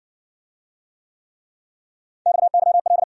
Morse code, heard as a keyed tone, about 56 wpm, 700 Hz. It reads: XYL